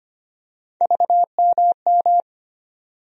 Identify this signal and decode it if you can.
Morse code, heard as a keyed tone, about 25 wpm, 700 Hz. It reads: VMM
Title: USA Call Sign Postfixes - 3 Letters 25wpm